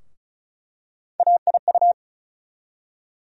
Morse code, heard as a keyed tone, about 36 wpm, 700 Hz. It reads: AIU
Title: USA Call Sign Postfixes - 3 Letters 35wpm